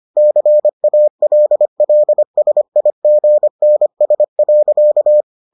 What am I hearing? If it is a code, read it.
Morse code, heard as a keyed tone, about 25 wpm, 600 Hz. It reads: CALLSIGNS.